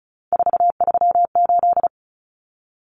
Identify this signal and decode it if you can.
Morse code, heard as a keyed tone, about 34 wpm, 700 Hz. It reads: 438
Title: Common Cut Numbers - 3-Digit 35wpm